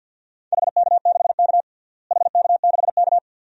Morse code, heard as a keyed tone, about 48 wpm, 700 Hz. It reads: HK6K HK6K
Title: DXCC-100 Call Signs - 4 Characters - Repeated 2x - 50wpm